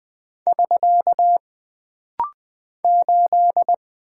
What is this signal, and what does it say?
Morse code, heard as a keyed tone, about 20 wpm, 700 Hz. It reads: <SK> 8
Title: Single Character (Letter:Number:Punctuation:Prosign) - Rapid-Fire - 20wpm